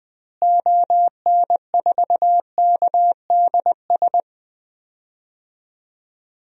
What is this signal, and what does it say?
Morse code, heard as a keyed tone, about 20 wpm, 700 Hz. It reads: ON4KDS
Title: DXCC-100 Call Signs - 6 Characters - Repeated 3x - 20wpm